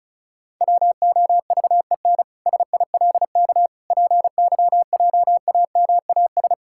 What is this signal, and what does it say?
Morse code, heard as a keyed tone, about 34 wpm, 700 Hz. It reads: WOVEN SILK PYJAMAS
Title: Pangram Challenge v2 - 35wpm